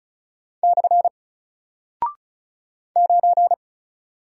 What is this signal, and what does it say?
Morse code, heard as a keyed tone, about 36 wpm, 700 Hz. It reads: / 9
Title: Single Character (Letter:Number:Punctuation:Prosign) Alt 35wpm